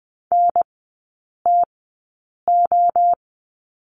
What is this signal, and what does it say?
Morse code, heard as a keyed tone, about 20 wpm, spaced as for 8 wpm, 700 Hz. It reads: NTO
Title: 060 Lesson 11e - Letter h - Three Characters 20wpm